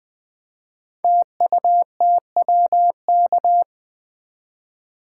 Morse code, heard as a keyed tone, about 20 wpm, 700 Hz. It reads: TUTWK
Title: Code Groups - 5 Letters - 20wpm